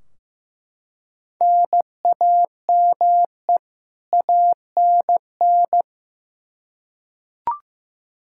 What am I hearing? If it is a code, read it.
Morse code, heard as a keyed tone, about 15 wpm, 700 Hz. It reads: NAME ANN
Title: K1USN SST QSO Practice 15wpm